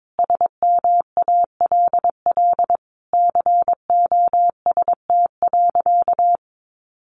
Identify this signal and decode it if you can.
Morse code, heard as a keyed tone, about 22 wpm, 700 Hz. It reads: SMALL COST.
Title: Sentences from Top 900 Words 22wpm